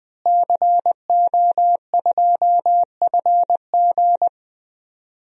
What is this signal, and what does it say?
Morse code, heard as a keyed tone, about 20 wpm, 700 Hz. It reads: CO2FG
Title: DXCC-100 Call Signs - 5 Characters - Repeated 2x - 20wpm